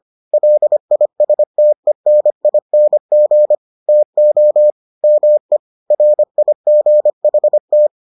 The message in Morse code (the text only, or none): LISTENING TO ME RIGHT